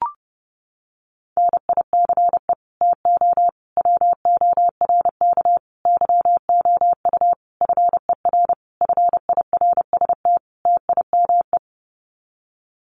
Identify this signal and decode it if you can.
Morse code, heard as a keyed tone, about 30 wpm, 700 Hz. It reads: NICE TO WORK YOU FER FIRST TIME
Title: Common QSO Phrases 30wpm